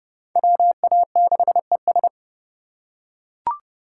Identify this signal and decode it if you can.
Morse code, heard as a keyed tone, about 29 wpm, 700 Hz. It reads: WA6ES
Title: USA Call Signs - 3x Character Spacing - 30wpm